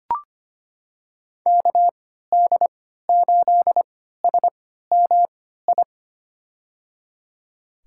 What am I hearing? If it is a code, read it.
Morse code, heard as a keyed tone, about 25 wpm, spaced as for 14 wpm, 700 Hz. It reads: KD8SMI
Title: USA Call Signs - 3x Character Spacing - 25wpm